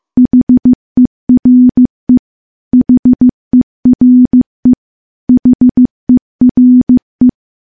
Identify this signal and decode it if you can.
Morse code, heard as a keyed tone, about 15 wpm, 250 Hz. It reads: HERE HERE HERE
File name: morse / morse-training-15WPM